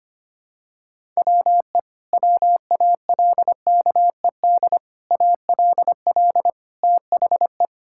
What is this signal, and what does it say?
Morse code, heard as a keyed tone, about 25 wpm, 700 Hz. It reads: WE WALKED ALL THE
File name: practice / 25wpm / Sentences with 4-Word Phrases - 25wpm